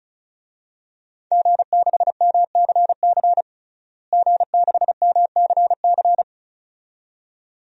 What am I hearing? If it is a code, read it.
Morse code, heard as a keyed tone, about 36 wpm, 700 Hz. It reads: G6MCC G6MCC
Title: DXCC-100 Call Signs - 5 Characters - Repeated 2x - 35wpm